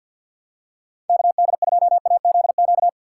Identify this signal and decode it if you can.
Morse code, heard as a keyed tone, about 47 wpm, 700 Hz. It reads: KD2AZQ
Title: USA Call Signs - 6 Characters - 6x Character Spacing - 50wpm